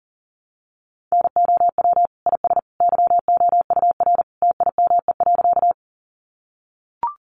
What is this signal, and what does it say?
Morse code, heard as a keyed tone, about 40 wpm, 700 Hz. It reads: NOW IS YOUR TIME.
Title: Sentences from Top 100 Words 40wpm